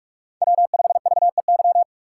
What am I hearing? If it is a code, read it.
Morse code, heard as a keyed tone, about 42 wpm, 700 Hz. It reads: W5VEY